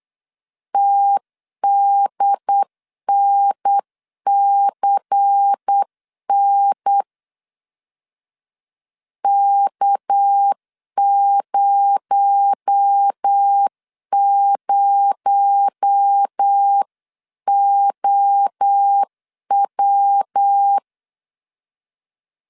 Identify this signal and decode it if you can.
Morse code, heard as a keyed tone, about 8 wpm, 800 Hz. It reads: TDNCN K00OW
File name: CW / DET / 1 6 1500C S C 8wpm 5219